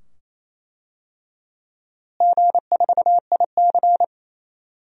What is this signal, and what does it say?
Morse code, heard as a keyed tone, about 28 wpm, 700 Hz. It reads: G4IC
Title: DXCC-100 Call Signs - 4 Characters - Repeated 2x - 28wpm